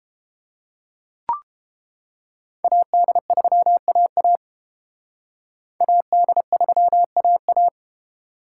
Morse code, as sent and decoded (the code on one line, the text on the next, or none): .- -.. ...-- .- .- / .- -.. ...-- .- .-
AD3AA AD3AA